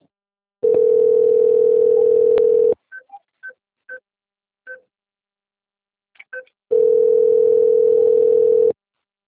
Ringback tone (North America), 440 + 480 Hz.